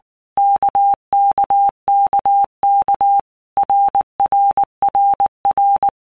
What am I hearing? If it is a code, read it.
Morse code, heard as a keyed tone, about 19 wpm, 800 Hz. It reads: KKKK RRRR
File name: All Groups 1-minute Blocks / Groups@18 / RKPX